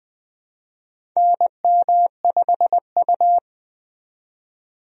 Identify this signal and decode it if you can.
Morse code, heard as a keyed tone, about 20 wpm, 700 Hz. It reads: NM5U
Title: USA Call Signs - 4 Characters - 20wpm